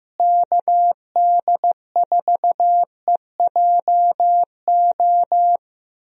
Morse code, heard as a keyed tone, about 15 wpm, 700 Hz. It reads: KD4EJO
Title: USA Call Signs - 6 Characters - Repeated 3x - 15wpm